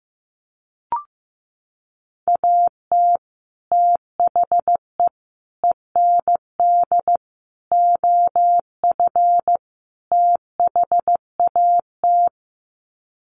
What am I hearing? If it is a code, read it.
Morse code, heard as a keyed tone, about 14 wpm, 700 Hz. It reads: AT THE END OF THAT
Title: Sets of 5 Words Encore2 15wpm